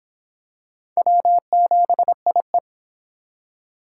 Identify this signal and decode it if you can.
Morse code, heard as a keyed tone, about 26 wpm, 700 Hz. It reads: W7IE